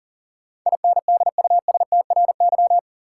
Morse code, heard as a keyed tone, about 40 wpm, 700 Hz. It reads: INDUSTRY
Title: Sentences from Top 1000 Words - Review Part 4 - New Word Drill - 40wpm